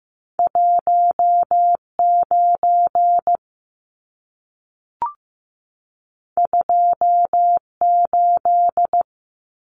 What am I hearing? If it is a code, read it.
Morse code, heard as a keyed tone, about 15 wpm, 700 Hz. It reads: 19 28
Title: Numbers - 2-Digit 15wpm